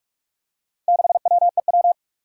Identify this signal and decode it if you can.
Morse code, heard as a keyed tone, about 46 wpm, 700 Hz. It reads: 6WEW
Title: Four Challenging Letter-Number Combinations 45wpm